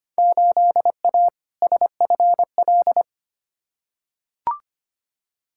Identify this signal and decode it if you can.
Morse code, heard as a keyed tone, about 25 wpm, 700 Hz. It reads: 8A SFL
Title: Field Day Exchanges Encore - 25wpm